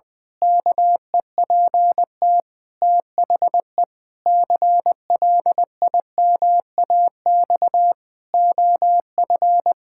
Morse code, heard as a keyed tone, about 21 wpm, 700 Hz. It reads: KEPT THE CLIMAX OF
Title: Pangram Challenge v2 - 20wpm